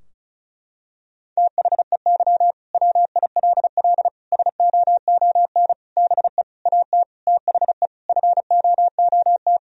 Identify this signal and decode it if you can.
Morse code, heard as a keyed tone, about 36 wpm, 700 Hz. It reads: THEY WILL SOON BE AT THE FOOT
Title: Sentences from Top 1000 Words - Review Part 4 - 35wpm